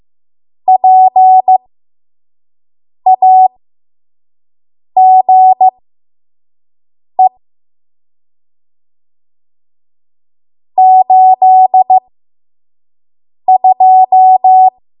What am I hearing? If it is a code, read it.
Morse code, heard as a keyed tone, about 15 wpm, spaced as for 5 wpm, 750 Hz. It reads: PAGE 82
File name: Morse / 05WPM